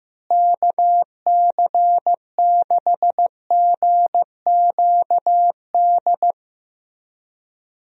Morse code, heard as a keyed tone, about 15 wpm, 700 Hz. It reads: KC6GQD
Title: USA Call Signs - 6 Characters - Repeated 2x - 15wpm